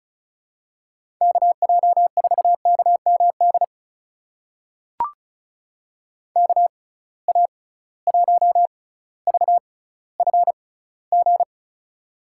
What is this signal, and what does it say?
Morse code, heard as a keyed tone, about 36 wpm, spaced as for 18 wpm, 700 Hz. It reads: KJ4KMD KA1VFG
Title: USA Call Signs - 6 Characters - 6x Character Spacing - 35wpm